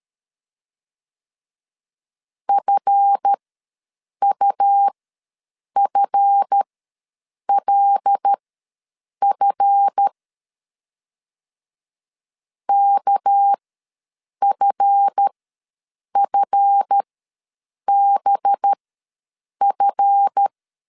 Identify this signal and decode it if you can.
Morse code, heard as a keyed tone, about 13 wpm, spaced as for 7 wpm, 800 Hz. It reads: FUFLF KFFBF
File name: LEZIONI / 7 GF9Z / 7 GF ws 12wpm 3140